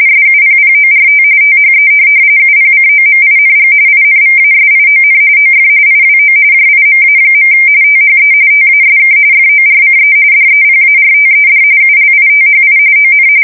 RTTY, about 45.45 baud. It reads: SUPPORT THE UPCOMING INVASION OF IWO JIMA. THE TASK FORCE DEPARTED ULITHI ON
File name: fsk170 / USS-Massachusetts